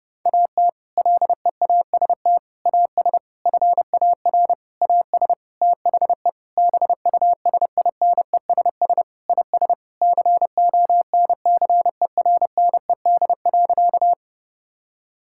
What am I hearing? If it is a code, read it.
Morse code, heard as a keyed tone, about 30 wpm, 700 Hz. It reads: AT LEAST AS FAR AS THE BUSINESS IS CONCERNED.
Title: Sentences from Top 1000 Words - Review Part 6 - 30wpm